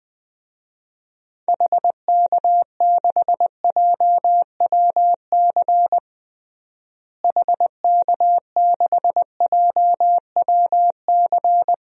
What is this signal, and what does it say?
Morse code, heard as a keyed tone, about 20 wpm, 700 Hz. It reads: HK6JWC HK6JWC
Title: DXCC-100 Call Signs - 6 Characters - Repeated 3x - 20wpm